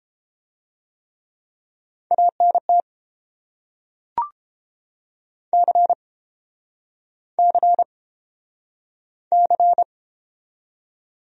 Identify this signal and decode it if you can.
Morse code, heard as a keyed tone, about 32 wpm, 700 Hz. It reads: ANT C C C